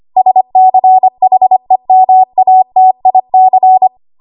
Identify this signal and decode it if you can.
Morse code, heard as a keyed tone, about 25 wpm, 750 Hz. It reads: SCHEMATIC
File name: Archive / 25 WPM / 25WPM